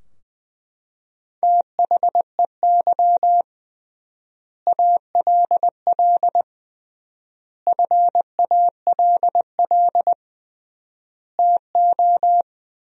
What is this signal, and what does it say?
Morse code, heard as a keyed tone, about 20 wpm, 700 Hz. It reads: THEY ALL FALL TO
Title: 3x Word-Spacing - Sentences from Top 500 Words - 20wpm